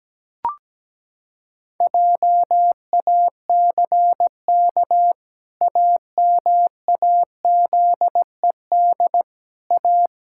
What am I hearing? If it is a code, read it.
Morse code, heard as a keyed tone, about 17 wpm, 700 Hz. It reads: JACK AMAZED A
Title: Pangram Challenge v2 - 17wpm